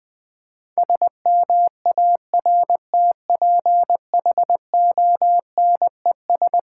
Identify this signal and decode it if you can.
Morse code, heard as a keyed tone, about 20 wpm, 700 Hz. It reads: SMARTPHONES
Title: Difficult Sentences with Hints - 20wpm